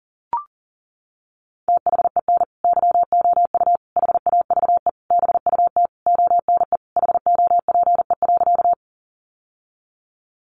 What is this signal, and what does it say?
Morse code, heard as a keyed tone, about 37 wpm, 700 Hz. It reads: THEN YOU HAVE BUT ONE HOPE.
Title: Sentences from Top 1000 Words 40wpm